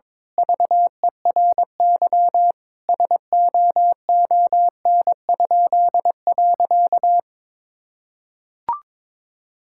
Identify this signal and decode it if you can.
Morse code, heard as a keyed tone, about 22 wpm, 700 Hz. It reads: VERY SOON?.
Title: Pangram Challenge v2 - 22wpm